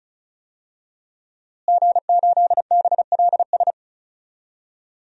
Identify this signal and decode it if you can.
Morse code, heard as a keyed tone, about 34 wpm, 700 Hz. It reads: G8BLS
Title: DXCC-100 Call Signs - 5 Characters - Repeated 2x - 35wpm